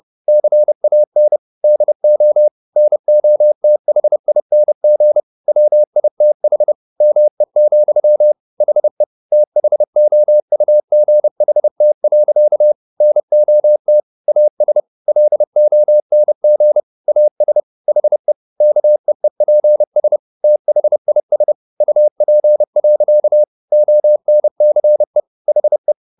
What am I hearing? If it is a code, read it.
Morse code, heard as a keyed tone, about 30 wpm, 600 Hz. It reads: CAN DO NOTHING WITH ME, HE THOUGHT. NOT AS LONG AS HE KEEPS THIS UP. ONCE HE
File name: the old man and the sea / the old man and the sea 1 30wpm